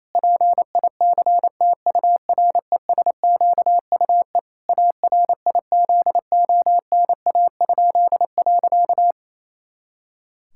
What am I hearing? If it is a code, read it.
Morse code, heard as a keyed tone, about 28 wpm, 700 Hz. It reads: PICTURESQUE ARIZONA?.
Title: Pangram Challenge v2 - 28wpm